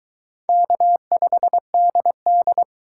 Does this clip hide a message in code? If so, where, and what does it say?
Yes: Morse code from 0.5 to 2.6 s, heard as a keyed tone, about 22 wpm, 700 Hz. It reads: K5DD